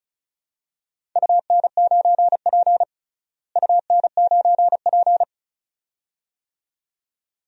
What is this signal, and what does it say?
Morse code, heard as a keyed tone, about 34 wpm, 700 Hz. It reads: UN9P UN9P
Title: DXCC-100 Call Signs - 4 Characters - Repeated 2x - 35wpm